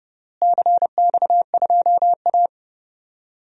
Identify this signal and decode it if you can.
Morse code, heard as a keyed tone, about 32 wpm, 700 Hz. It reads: CX2A
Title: DXCC-100 Call Signs - 4 Characters - Repeated 3x - 30wpm